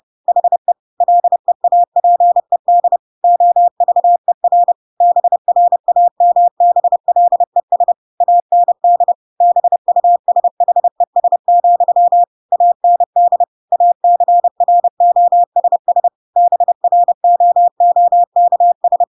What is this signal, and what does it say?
Morse code, heard as a keyed tone, about 30 wpm, 700 Hz. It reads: HE LEAPED OVER BRAMBLES AND BUSHES, AND ACROSS BROOKS